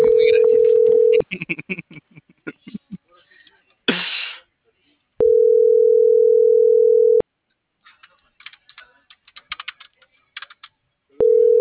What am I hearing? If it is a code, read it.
Ringback tone (North America), 440 + 480 Hz.